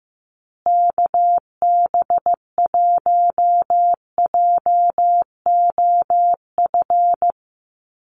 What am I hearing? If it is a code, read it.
Morse code, heard as a keyed tone, about 15 wpm, 700 Hz. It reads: KB1JOF
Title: USA Call Signs Repeated 2x 15wpm